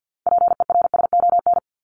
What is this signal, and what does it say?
Morse code, heard as a keyed tone, about 48 wpm, 700 Hz. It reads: PERSON
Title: ES:S Suffix - Top 100 Words with Base Words 50wpm